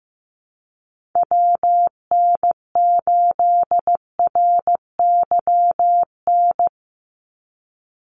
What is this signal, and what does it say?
Morse code, heard as a keyed tone, about 15 wpm, 700 Hz. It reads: WN8RYN
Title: USA Call Signs 15wpm